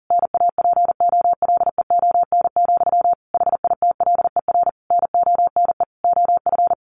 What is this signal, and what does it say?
Morse code, heard as a keyed tone, about 40 wpm, 700 Hz. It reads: NAPOLEON, HITLER NONE OF
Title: A Sound Of Thunder 40wpm